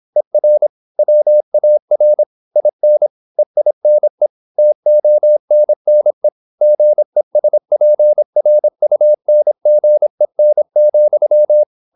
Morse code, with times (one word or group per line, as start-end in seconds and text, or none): 0.2-0.7 ER
1.0-2.2 WAR
2.6-3.1 IN
3.4-4.3 EINE
4.6-6.3 TONNE
6.6-11.6 GESPRUNGEN,